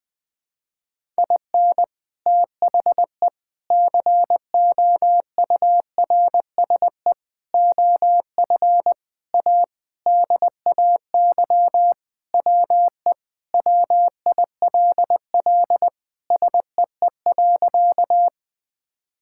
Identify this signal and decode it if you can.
Morse code, heard as a keyed tone, about 20 wpm, 700 Hz. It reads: IN THE COURSE OF A DAY WE WILL SEE.
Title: Sentences with 3-Word Phrases - 20wpm